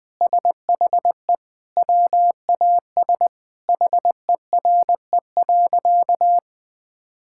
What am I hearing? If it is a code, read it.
Morse code, heard as a keyed tone, about 21 wpm, 700 Hz. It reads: SHE WAS HERE.
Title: Sentences with 3-Word Phrases - 20wpm